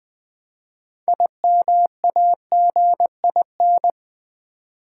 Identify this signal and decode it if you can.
Morse code, heard as a keyed tone, about 20 wpm, 700 Hz. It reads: IMAGIN
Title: Build A Word - 3 Letters At A Time - 9-Letter Words - 20wpm